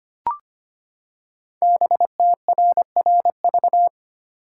Morse code, heard as a keyed tone, about 25 wpm, 700 Hz. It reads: BTRRV
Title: Code Groups - 5 Letters - 25wpm